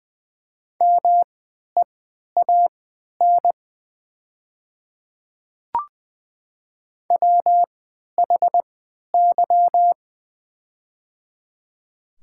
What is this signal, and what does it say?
Morse code, heard as a keyed tone, about 20 wpm, spaced as for 11 wpm, 700 Hz. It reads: MEAN WHY
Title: 3x Character Spacing - Top 200 Words - 20wpm